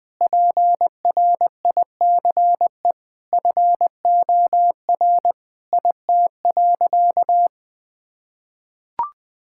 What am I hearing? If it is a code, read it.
Morse code, heard as a keyed tone, about 19 wpm, 700 Hz. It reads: PRICE FOR IT.
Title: Sentences from Top 1000 Words - Review Part 6 - 20wpm